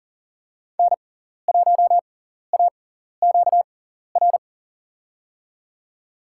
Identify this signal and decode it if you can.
Morse code, heard as a keyed tone, about 40 wpm, spaced as for 14 wpm, 700 Hz. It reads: N1AQR
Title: USA Call Signs - 5 Characters - 6x Character Spacing - 40wpm